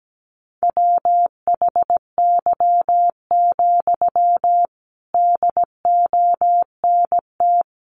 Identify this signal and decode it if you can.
Morse code, heard as a keyed tone, about 17 wpm, 700 Hz. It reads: WHY, DONT
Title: Sentences from Top 300 Words 17wpm